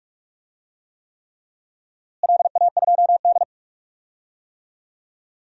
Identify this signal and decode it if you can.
Morse code, heard as a keyed tone, about 44 wpm, 700 Hz. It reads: LA2D
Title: DXCC-100 Call Signs - 4 Characters - Repeated 3x - 45wpm